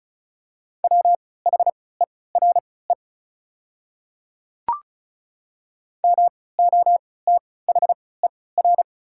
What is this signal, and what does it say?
Morse code, heard as a keyed tone, about 36 wpm, spaced as for 20 wpm, 700 Hz. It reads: WHERE MOTHER
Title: 3x Character Spacing - Top 200 Words - 35wpm